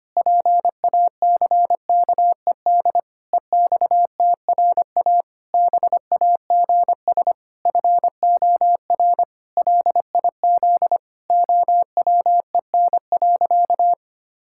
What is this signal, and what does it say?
Morse code, heard as a keyed tone, about 25 wpm, 700 Hz. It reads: PACKED EXTRA BAGS FOR LIZ OWEN.
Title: Pangram Challenge v2 - 25wpm